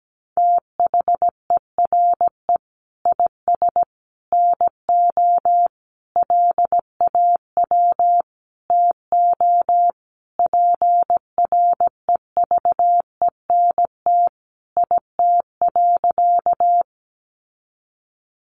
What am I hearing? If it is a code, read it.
Morse code, heard as a keyed tone, about 17 wpm, 700 Hz. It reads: THERE IS NO LAW TO PREVENT IT.
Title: Sentences from Top 1100 Words 17wpm